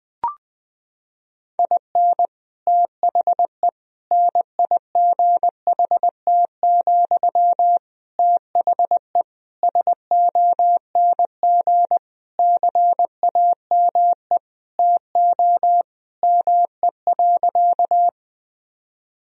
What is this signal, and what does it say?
Morse code, heard as a keyed tone, about 20 wpm, 700 Hz. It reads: IN THE NIGHT, THE SONG CAME TO ME.
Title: Sentences from Top 1000 Words - Review Part 2 - 20wpm